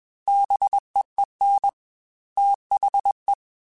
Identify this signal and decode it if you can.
Morse code, heard as a keyed tone, about 21 wpm, 800 Hz. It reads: BEEN THE
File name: Technical Subjects / Technical Subjects-mp3s / AND Gate @20